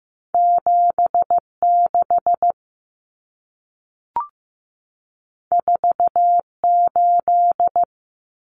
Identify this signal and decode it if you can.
Morse code, heard as a keyed tone, about 15 wpm, 700 Hz. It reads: 76 48
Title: Common Cut Numbers - 2-Digit 15wpm